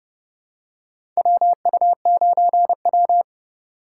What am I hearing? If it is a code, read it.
Morse code, heard as a keyed tone, about 30 wpm, 700 Hz. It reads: WU9W
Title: USA Call Signs - 4 Characters - 3x Character Spacing - 30wpm